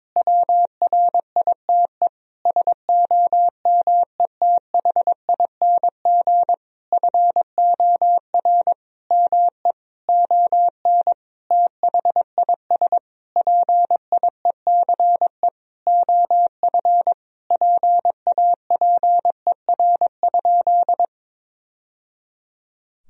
Morse code, heard as a keyed tone, about 22 wpm, 700 Hz. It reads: WRITE SOMETHING FOR ME ON THIS PIECE OF PAPER?
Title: Sentences from Top 1000 Words - Review Part 2 - 22wpm